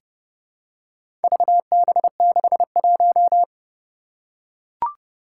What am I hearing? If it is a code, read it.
Morse code, heard as a keyed tone, about 30 wpm, 700 Hz. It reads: VB61
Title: Four Challenging Letter-Number Combinations 30wpm